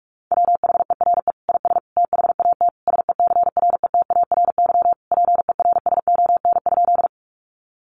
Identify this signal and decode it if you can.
Morse code, heard as a keyed tone, about 44 wpm, 700 Hz. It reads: WHERE IS THAT SECRETARY PERSON?
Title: Sentences from Top 1300 Words 45wpm